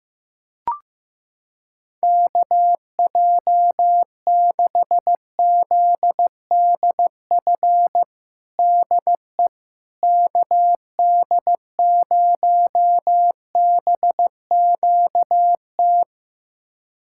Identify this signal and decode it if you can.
Morse code, heard as a keyed tone, about 16 wpm, 700 Hz. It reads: KJ6ZDF DE KD0BQT
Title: USA Callsign DE Callsign - 6 Characters - 15wpm